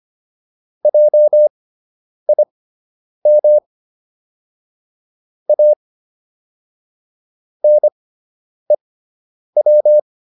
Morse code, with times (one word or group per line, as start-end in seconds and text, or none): 0.8-3.6 JIM
5.5-5.7 A
7.6-10.0 NEW